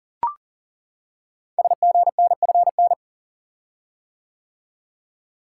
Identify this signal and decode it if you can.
Morse code, heard as a keyed tone, about 38 wpm, 690 Hz. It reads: SGNFN
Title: Code Groups - 5 Letters - 40wpm